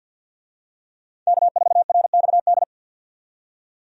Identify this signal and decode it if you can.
Morse code, heard as a keyed tone, about 50 wpm, 700 Hz. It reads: K4RXD
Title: USA Call Signs - 5 Characters - 50wpm